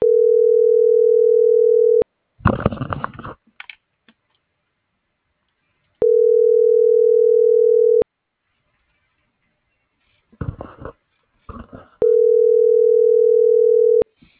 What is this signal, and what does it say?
Ringback tone (North America), 440 + 480 Hz.